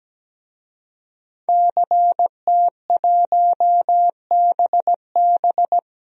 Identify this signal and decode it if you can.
Morse code, heard as a keyed tone, about 17 wpm, 700 Hz. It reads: CT1BB
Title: DXCC-100 Call Signs - 5 Characters - 17wpm